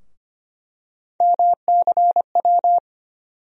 Morse code, heard as a keyed tone, about 25 wpm, 700 Hz. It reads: MCW
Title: USA Call Sign Postfixes - 3 Letters 25wpm